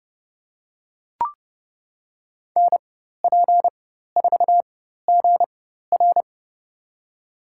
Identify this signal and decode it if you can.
Morse code, heard as a keyed tone, about 30 wpm, spaced as for 14 wpm, 700 Hz. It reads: NP4GR
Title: USA Call Signs - 5 Characters - 4x Character Spacing - 30wpm